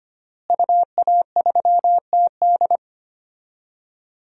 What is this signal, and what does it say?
Morse code, heard as a keyed tone, about 25 wpm, 700 Hz. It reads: UA3TD
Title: DXCC-100 Call Signs - 5 Characters - Repeated 3x - 25wpm